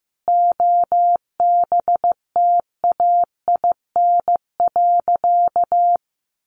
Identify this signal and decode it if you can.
Morse code, heard as a keyed tone, about 15 wpm, 700 Hz. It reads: OBTAIN.
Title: Sentences from Top 1300 Words 15wpm